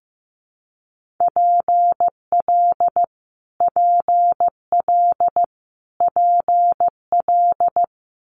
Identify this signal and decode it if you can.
Morse code, heard as a keyed tone, about 15 wpm, 700 Hz. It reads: PL PL PL
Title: Multi-letter Phonemes Repeated 3x 15wpm